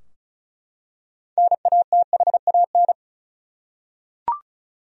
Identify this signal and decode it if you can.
Morse code, heard as a keyed tone, about 36 wpm, 700 Hz. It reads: NATHAN
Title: USA Common Names - 35wpm